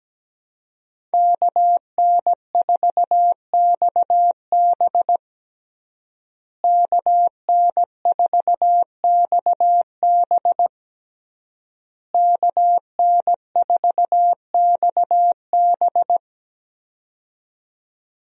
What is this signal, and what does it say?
Morse code, heard as a keyed tone, about 17 wpm, 700 Hz. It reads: KN4XB KN4XB KN4XB
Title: USA Call Signs - 5 Characters - Repeated 3x - 17wpm